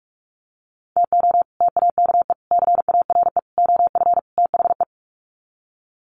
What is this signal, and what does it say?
Morse code, heard as a keyed tone, about 44 wpm, 700 Hz. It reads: TO TAKE CARE OF THE
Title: Sets of 5 Words Encore 45wpm